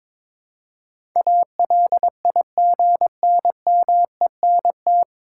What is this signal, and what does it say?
Morse code, heard as a keyed tone, about 23 wpm, 700 Hz. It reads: ALIGNMENT
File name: 9 Letter Words - First 3 Letters Given - 22wpm